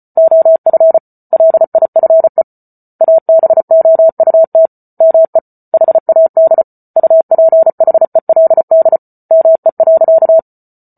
Morse code, heard as a keyed tone, about 34 wpm, 650 Hz. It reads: OF LIFE ABOUT ME HAD UPHELD ME.